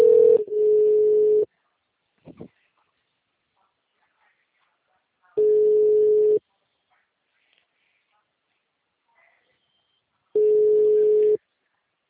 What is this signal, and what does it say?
Ringback tone (Europe, 425 Hz), with speech.
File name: out- -2011- - - .1523